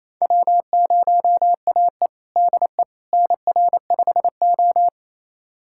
Morse code, heard as a keyed tone, about 28 wpm, 700 Hz. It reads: W0AE DE NR5O
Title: USA Callsign DE Callsign - 4 Characters - 28wpm